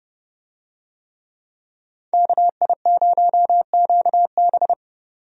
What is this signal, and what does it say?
Morse code, heard as a keyed tone, about 30 wpm, 700 Hz. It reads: KI0QB
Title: USA Call Signs - 5 Characters - 30wpm